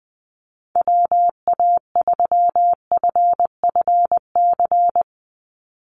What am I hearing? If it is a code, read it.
Morse code, heard as a keyed tone, about 20 wpm, 700 Hz. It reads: WA3FFC
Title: USA Call Signs 20wpm